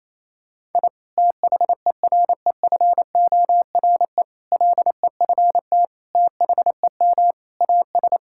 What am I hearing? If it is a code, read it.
Morse code, heard as a keyed tone, about 28 wpm, 700 Hz. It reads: I THEREFORE LEFT THEM AS
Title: Sentences from Top 1000 Words - Review Part 5 - 28wpm